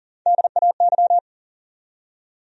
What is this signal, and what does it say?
Morse code, heard as a keyed tone, about 40 wpm, 700 Hz. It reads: DAY
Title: Top 100 Words 40wpm